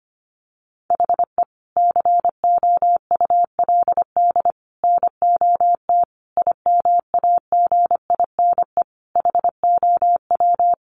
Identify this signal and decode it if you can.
Morse code, heard as a keyed tone, about 25 wpm, 700 Hz. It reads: HE COULD NOT IMAGINE HOW
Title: Sentences from Top 1400 Words 25wpm